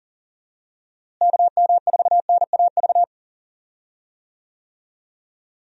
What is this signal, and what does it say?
Morse code, heard as a keyed tone, about 40 wpm, 700 Hz. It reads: KM4NAV
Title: USA Call Signs - 6 Characters - Repeated 3x - 40wpm